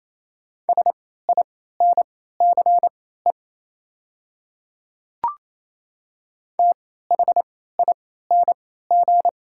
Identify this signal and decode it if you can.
Morse code, heard as a keyed tone, about 28 wpm, spaced as for 16 wpm, 700 Hz. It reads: SINCE THING
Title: 3x Character Spacing - Top 300 Words - 28wpm